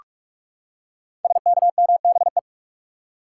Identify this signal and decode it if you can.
Morse code, heard as a keyed tone, about 44 wpm, 700 Hz. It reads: SKMBE
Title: Code Groups - 5 Letters - 45wpm